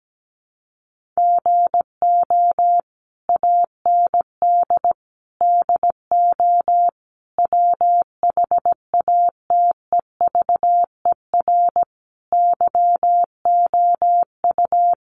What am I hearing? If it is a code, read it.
Morse code, heard as a keyed tone, about 17 wpm, 700 Hz. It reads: GO AND DO WHATEVER YOU
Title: Sentences from Top 1100 Words 17wpm